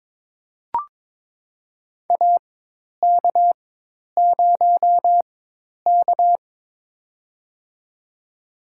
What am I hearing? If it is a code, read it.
Morse code, heard as a keyed tone, about 23 wpm, spaced as for 10 wpm, 700 Hz. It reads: AK0K
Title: USA Call Signs - 4 Characters - 4x Character Spacing - 22wpm